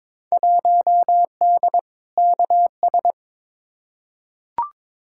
Morse code, heard as a keyed tone, about 22 wpm, 700 Hz. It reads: 1D KS
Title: Field Day Exchanges - 22wpm